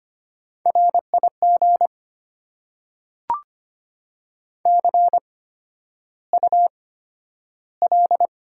Morse code, heard as a keyed tone, about 25 wpm, spaced as for 11 wpm, 700 Hz. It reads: RIG CUL
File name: Top 90 QSO Elements - 8x Character Spacing - 25wpm